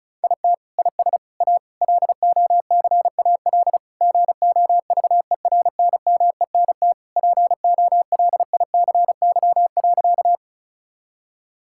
Morse code, heard as a keyed tone, about 34 wpm, 700 Hz. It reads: IT IS A LOCAL GOVERNMENT POLICY.
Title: Sentences from Top 1000 Words - Review Part 5 - 35wpm